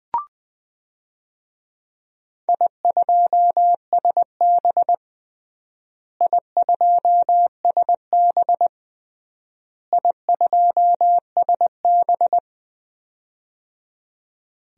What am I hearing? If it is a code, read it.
Morse code, heard as a keyed tone, about 20 wpm, 700 Hz. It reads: I2SB I2SB I2SB
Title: DXCC-100 Call Signs - 4 Characters - Repeated 3x - 20wpm